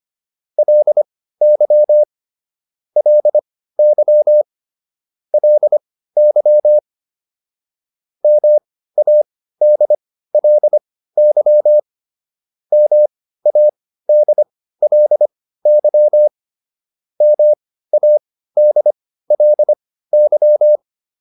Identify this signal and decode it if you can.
Morse code, heard as a keyed tone, about 25 wpm, spaced as for 15 wpm, 600 Hz. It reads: LY LY LY MADLY MADLY MADLY